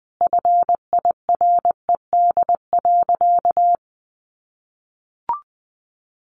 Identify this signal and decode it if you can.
Morse code, heard as a keyed tone, about 20 wpm, 700 Hz. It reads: FIRED.
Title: A Sound Of Thunder 20wpm